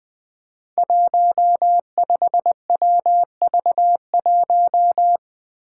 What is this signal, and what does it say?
Morse code, heard as a keyed tone, about 20 wpm, 700 Hz. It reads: 15WV1
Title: Five Challenging Letter-Number Combinations 20wpm